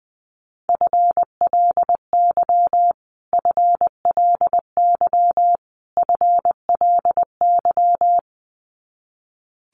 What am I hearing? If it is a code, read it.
Morse code, heard as a keyed tone, about 20 wpm, 700 Hz. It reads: FLY FLY FLY
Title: Top 500 Words Repeated 20wpm